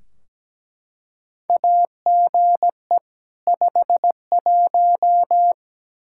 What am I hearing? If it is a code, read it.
Morse code, heard as a keyed tone, about 17 wpm, 700 Hz. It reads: AGE 51
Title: Practice QSO - Exchange 3 - More Info - 17wpm